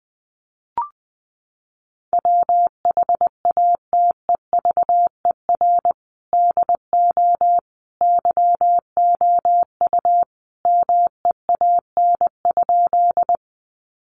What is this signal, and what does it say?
Morse code, heard as a keyed tone, about 20 wpm, 700 Hz. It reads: WHATEVER DO YOU MEAN?
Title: Sentences from Top 1100 Words 20wpm